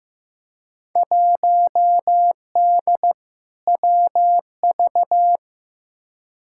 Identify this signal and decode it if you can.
Morse code, heard as a keyed tone, about 15 wpm, 700 Hz. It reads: 1D WV
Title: Field Day Exchanges - 15wpm